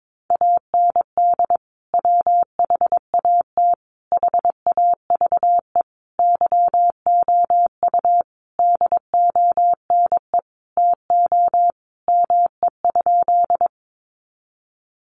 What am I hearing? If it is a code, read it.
Morse code, heard as a keyed tone, about 22 wpm, 700 Hz. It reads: AND WHAT HAVE YOU DONE TO ME?
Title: Sentences from Top 500 Words 22wpm